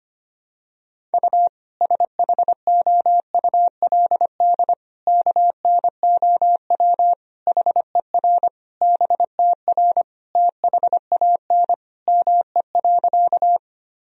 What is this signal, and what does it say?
Morse code, heard as a keyed tone, about 25 wpm, 700 Hz. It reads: U SHOULD KNOW HER BTR THAN ME.
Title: Sentences from Top 400 Words and Abbreviations 25wpm